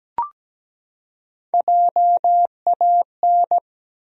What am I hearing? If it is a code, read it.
Morse code, heard as a keyed tone, about 17 wpm, 700 Hz. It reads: JAN
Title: USA Common Names - Max 3 Letters - 17wpm